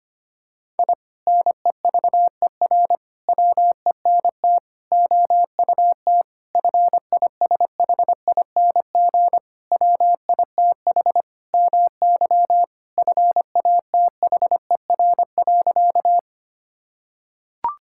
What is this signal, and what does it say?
Morse code, heard as a keyed tone, about 25 wpm, 700 Hz. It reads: I NEVER WENT OUT FISHING WITH MY FATHER.
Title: Sentences from Top 1000 Words - Review Part 5 - 25wpm